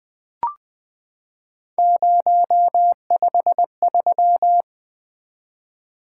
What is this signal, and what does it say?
Morse code, heard as a keyed tone, about 20 wpm, 700 Hz. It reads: 053
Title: Three Letter-Number Combinations - 20wpm